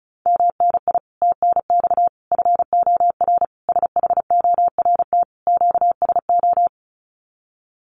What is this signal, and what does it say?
Morse code, heard as a keyed tone, about 34 wpm, 700 Hz. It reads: MNI TNX FOR SHORT QSO
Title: Common QSO Phrases 35wpm